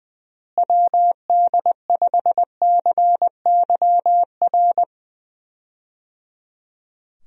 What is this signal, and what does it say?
Morse code, heard as a keyed tone, about 20 wpm, 700 Hz. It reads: WD5CYR
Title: USA Call Signs - 6 Characters - Repeated 3x - 20wpm